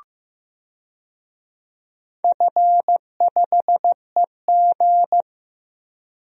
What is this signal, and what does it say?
Morse code, heard as a keyed tone, about 15 wpm, 700 Hz. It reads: F5EG
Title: DXCC-100 Call Signs - 4 Characters - 15wpm